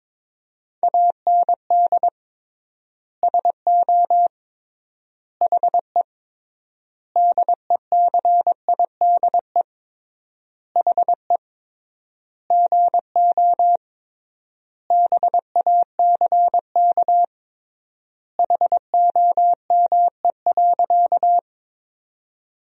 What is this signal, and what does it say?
Morse code, heard as a keyed tone, about 22 wpm, 700 Hz. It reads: AND SO HE DECIDE HE GO BACK HOME.
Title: 3x Word-Spacing - Sentences from Top 500 Words - 22wpm